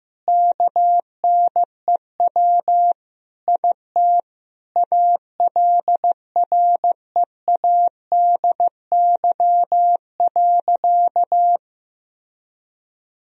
Morse code, heard as a keyed tone, about 14 wpm, 700 Hz. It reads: KNEW IT ALREADY.
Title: Sentences from Top 1000 Words - Review Part 3 - 15wpm